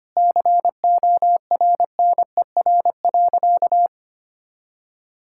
Morse code, heard as a keyed tone, about 25 wpm, 700 Hz. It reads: CORNER.
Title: Sentences from Top 1600 Words 25wpm